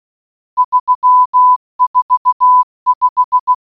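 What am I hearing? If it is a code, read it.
Morse code, heard as a keyed tone, about 16 wpm, 1000 Hz. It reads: 345
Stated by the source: Morse audio for a geocache puzzle, GCB8MYB Morse-Bonus Reloaded, in Baden-Württemberg, Germany, created by Wolli.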